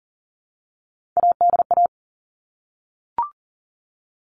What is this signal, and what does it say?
Morse code, heard as a keyed tone, about 40 wpm, 700 Hz. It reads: ADA
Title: All Cut Numbers - 3-Digit 40wpm